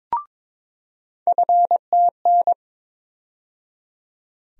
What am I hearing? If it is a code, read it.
Morse code, heard as a keyed tone, about 22 wpm, 700 Hz. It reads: FTN
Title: USA Call Sign Postfixes - 3 Letters 22wpm